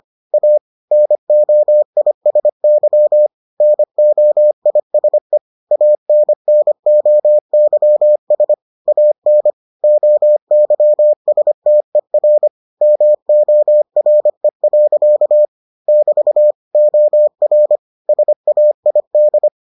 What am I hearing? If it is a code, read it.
Morse code, heard as a keyed tone, about 25 wpm, 600 Hz. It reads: A NOISY NOISE ANNOYS AN OYSTER MORE. <BT> OR SAID